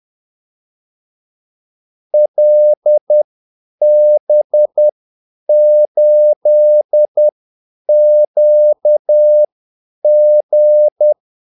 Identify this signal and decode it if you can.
Morse code, heard as a keyed tone, about 10 wpm, spaced as for 8 wpm, 600 Hz. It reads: LB8QG
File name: Callsigns-10-8wpm